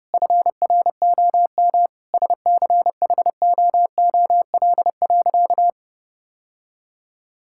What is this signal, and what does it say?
Morse code, heard as a keyed tone, about 32 wpm, 700 Hz. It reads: FROM SCHOOL.
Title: Sentences from Top 1000 Words - Review Part 6 - 30wpm